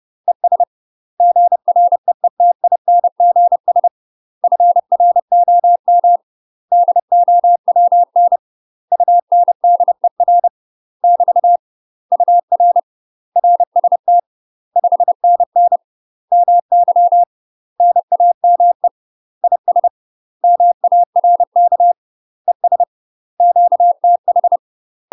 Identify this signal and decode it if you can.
Morse code, heard as a keyed tone, about 30 wpm, 700 Hz. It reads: ES GREETINGS FROM DOWN UNDER <BT> UR RST 5NN MY NAME IS MARK ES QTH
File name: UK MASTER QSOs TRAINING / 30WPM / UK MASTER QSOs 0029